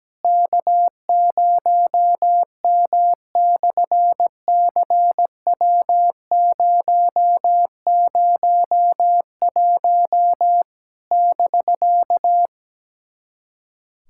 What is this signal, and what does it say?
Morse code, heard as a keyed tone, about 17 wpm, 700 Hz. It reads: K0M/CW001 <BK>
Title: SOTA QSO Practice 17wpm